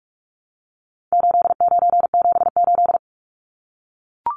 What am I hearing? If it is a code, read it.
Morse code, heard as a keyed tone, about 44 wpm, 700 Hz. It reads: 8978
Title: Numbers - 4-Digit 45wpm